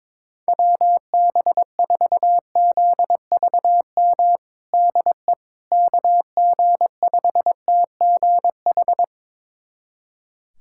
Morse code, heard as a keyed tone, about 22 wpm, 700 Hz. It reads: WB4ZVM DE KG5TGH
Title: USA Callsign DE Callsign - 6 Characters - 22wpm